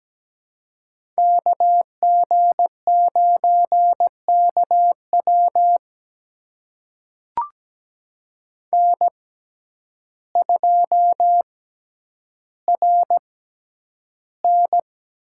Morse code, heard as a keyed tone, about 17 wpm, spaced as for 9 wpm, 700 Hz. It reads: KG9KW N2RN
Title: USA Call Signs - 6x Character Spacing - 17wpm